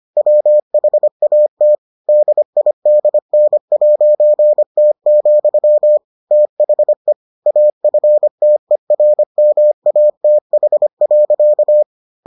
Morse code, heard as a keyed tone, about 25 wpm, 600 Hz. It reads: WHAT DIDN'T, THE AFTERMATH.